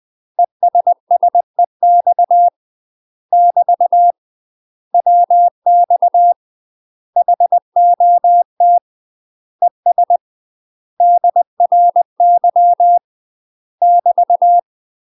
Morse code, heard as a keyed tone, about 20 wpm, 700 Hz. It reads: ESSEX <BT> WX HOT ES DRY <BT>
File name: UK MASTER QSOs TRAINING / 20WPM / UK MASTER QSOs 0052